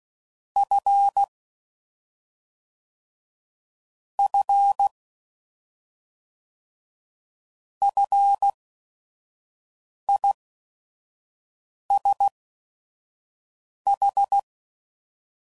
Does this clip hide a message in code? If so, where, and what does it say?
Yes: Morse code from 0.6 to 14.4 s, heard as a keyed tone, about 16 wpm, spaced as for 5 wpm, 800 Hz. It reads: F F FISH